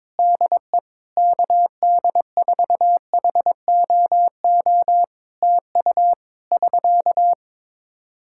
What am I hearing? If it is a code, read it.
Morse code, heard as a keyed tone, about 22 wpm, 700 Hz. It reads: DE KD4HOO TU <SK>
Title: Practice QSO - Exchange 4 - Ending - 22wpm